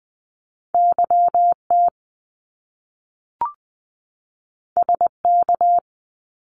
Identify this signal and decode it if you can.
Morse code, heard as a keyed tone, about 20 wpm, 700 Hz. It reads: YT SK
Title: Canadian Provinces and Territories 20wpm